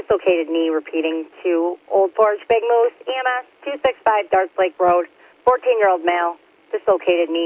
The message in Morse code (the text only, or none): none